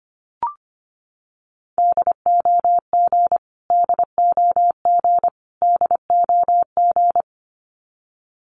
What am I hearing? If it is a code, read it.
Morse code, heard as a keyed tone, about 25 wpm, 700 Hz. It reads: DOG DOG DOG
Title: Top 500 Words Repeated 25wpm